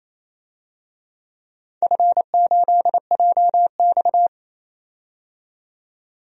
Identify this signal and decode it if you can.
Morse code, heard as a keyed tone, about 28 wpm, 700 Hz. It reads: F8JX
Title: DXCC-100 Call Signs - 4 Characters - 28wpm